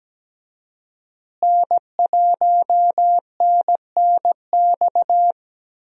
Morse code, heard as a keyed tone, about 17 wpm, 700 Hz. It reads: N1NNX
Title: USA Call Signs - 5 Characters - Repeated 3x - 17wpm